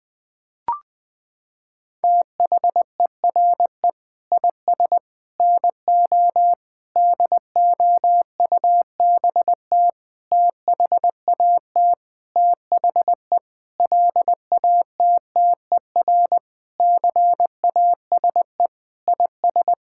Morse code, heard as a keyed tone, about 21 wpm, 700 Hz. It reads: THERE IS NO DOUBT THAT THE LATTER CASE IS
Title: Sentences from Top 1600 Words 20wpm